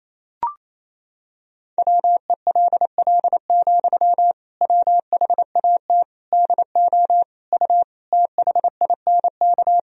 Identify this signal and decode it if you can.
Morse code, heard as a keyed tone, about 28 wpm, 700 Hz. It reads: WELL, WHAT DO U THINK
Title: Sentences from Top 200 Words and Abbreviations 28wpm